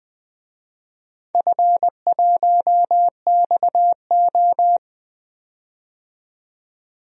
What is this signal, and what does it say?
Morse code, heard as a keyed tone, about 21 wpm, 700 Hz. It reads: F1XO
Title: DXCC-100 Call Signs - 4 Characters - 20wpm